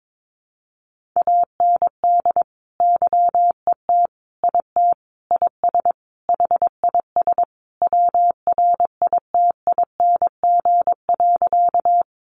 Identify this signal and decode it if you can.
Morse code, heard as a keyed tone, about 22 wpm, 700 Hz. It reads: AND YET IT IS HIS WRITING.
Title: Sentences from Top 800 Words 22wpm